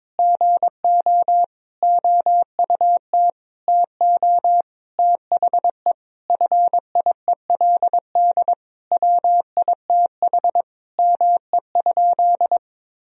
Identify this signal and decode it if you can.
Morse code, heard as a keyed tone, about 22 wpm, 700 Hz. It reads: GO OUT TO THE FIELD WITH ME?
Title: Sentences from Top 1000 Words - Review Part 2 - 22wpm